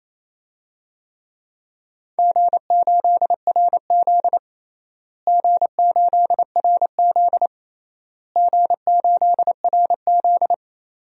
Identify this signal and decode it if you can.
Morse code, heard as a keyed tone, about 28 wpm, 700 Hz. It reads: G8RZ G8RZ G8RZ
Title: DXCC-100 Call Signs - 4 Characters - Repeated 3x - 28wpm